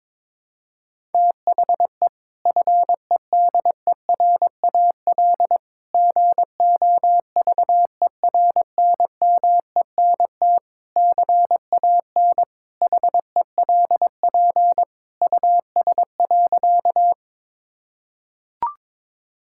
Morse code, heard as a keyed tone, about 22 wpm, 700 Hz. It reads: THE FEDERAL GOVERNMENT CAN HELP US.
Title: Sentences from Top 1000 Words - Review Part 5 - 22wpm